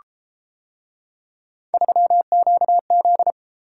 Morse code, heard as a keyed tone, about 31 wpm, 700 Hz. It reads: 3QZ